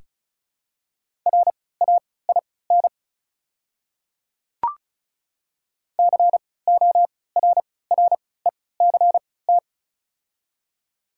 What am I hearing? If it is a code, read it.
Morse code, heard as a keyed tone, about 35 wpm, spaced as for 20 wpm, 700 Hz. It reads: RAIN CORRECT
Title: 3x Character Spacing - Top 500 Words - 35wpm